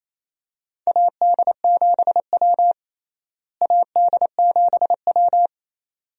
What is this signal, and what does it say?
Morse code, heard as a keyed tone, about 27 wpm, 700 Hz. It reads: AD7W AD7W
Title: USA Call Signs - 4 Characters - Repeated 2x - 28wpm